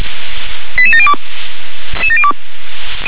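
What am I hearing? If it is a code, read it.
Five-tone selective call (ZVEI), heard as quick runs of five tones, tones of about 80 ms. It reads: 88702
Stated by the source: ZVEI five-tone selective call received on an SDR.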